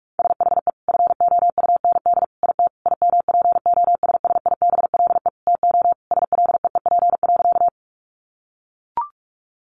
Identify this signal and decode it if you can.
Morse code, heard as a keyed tone, about 44 wpm, 700 Hz. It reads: SHE FOUND IT IMPOSSIBLE TO SLEEP.
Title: Sentences from Top 1100 Words 45wpm